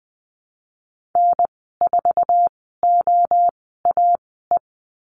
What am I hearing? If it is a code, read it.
Morse code, heard as a keyed tone, about 19 wpm, spaced as for 14 wpm, 700 Hz. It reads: N4OAE
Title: 055 Lesson 10g - Letter r - Call Signs 20wpm